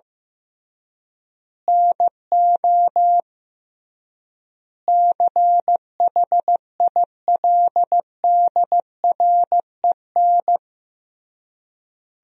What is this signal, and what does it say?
Morse code, heard as a keyed tone, about 15 wpm, 700 Hz. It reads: NO CHILDREN
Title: 3x Word-Spacing - Sentences from Top 400 Words - 15wpm